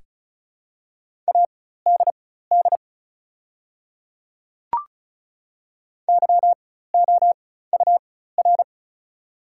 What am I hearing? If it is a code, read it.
Morse code, heard as a keyed tone, about 34 wpm, spaced as for 16 wpm, 700 Hz. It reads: ADD YOUR
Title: 4x Character Spacing - Top 200 Words - 35wpm